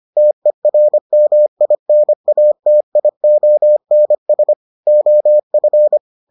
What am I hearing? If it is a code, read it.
Morse code, heard as a keyed tone, about 25 wpm, 600 Hz. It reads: TERMINATIONS OF